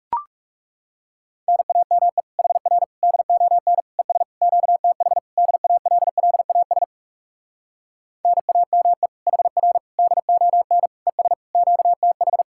Morse code, heard as a keyed tone, about 37 wpm, 700 Hz. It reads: NAME HR DON ES QTH DALLAS NAME HR DON ES QTH
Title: Speed-Racing Common QSO Phrases - 30wpm